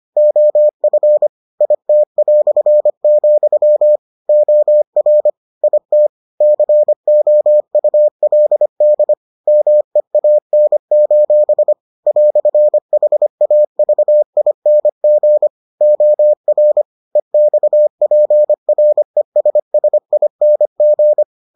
Morse code, heard as a keyed tone, about 25 wpm, 600 Hz. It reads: OF IT", OR IT COULD MEAN: "HAVING OR EXPRESSING